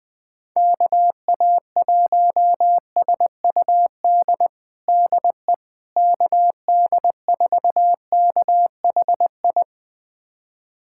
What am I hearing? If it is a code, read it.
Morse code, heard as a keyed tone, about 20 wpm, 700 Hz. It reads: KA1SUD DE KD4KHI
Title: USA Callsign DE Callsign - 6 Characters - 20wpm